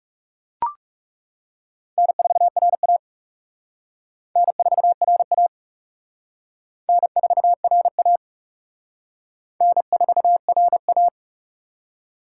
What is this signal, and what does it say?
Morse code, heard as a keyed tone, about 38 wpm, 700 Hz. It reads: N4RA N4RA N4RA N4RA